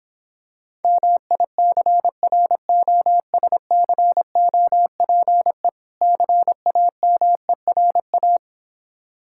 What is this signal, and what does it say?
Morse code, heard as a keyed tone, about 26 wpm, 700 Hz. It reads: MICROSCOPE CAMERA